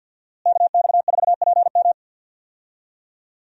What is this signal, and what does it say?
Morse code, heard as a keyed tone, about 52 wpm, 700 Hz. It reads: KX4PM
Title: USA Call Signs - 5 Characters - Repeated 3x - 50wpm